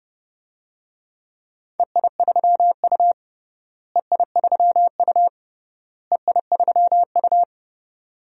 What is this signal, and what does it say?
Morse code, heard as a keyed tone, about 31 wpm, 700 Hz. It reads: EI3U EI3U EI3U
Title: DXCC-100 Call Signs - 4 Characters - Repeated 3x - 30wpm